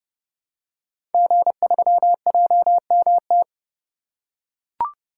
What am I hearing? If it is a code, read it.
Morse code, heard as a keyed tone, about 30 wpm, 700 Hz. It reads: G3JMT
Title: Code Groups - 5 Letter:Numbers - 30wpm